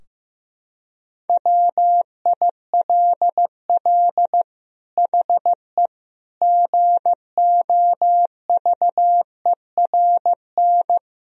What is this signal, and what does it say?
Morse code, heard as a keyed tone, about 15 wpm, 700 Hz. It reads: WILL HE GOVERN
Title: Sentences from Top 1000 Words - Review Part 2 - 15wpm